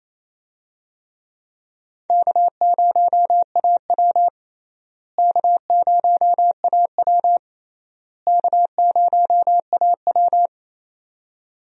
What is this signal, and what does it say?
Morse code, heard as a keyed tone, about 28 wpm, 700 Hz. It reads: K0AW K0AW K0AW
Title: USA Call Signs - 4 Characters - Repeated 3x - 28wpm